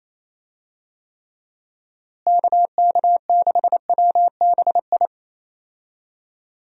Morse code, heard as a keyed tone, about 28 wpm, 700 Hz. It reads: KK6WBI
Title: USA Call Signs - 6 Characters - Repeated 3x - 28wpm